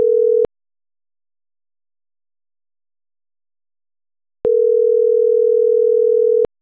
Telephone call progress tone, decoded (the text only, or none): ringback tone